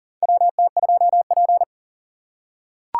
Morse code, heard as a keyed tone, about 40 wpm, 700 Hz. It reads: WT2P
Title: USA Call Signs - 4 Characters - 4x Character Spacing - 40wpm